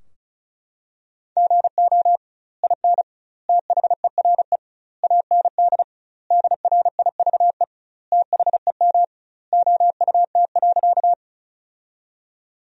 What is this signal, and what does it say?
Morse code, heard as a keyed tone, about 34 wpm, 700 Hz. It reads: GO IN THERE AND DRIVE THEM OUT.
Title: 2x Word-Spacing - Sentences from Top 500 Words - 35wpm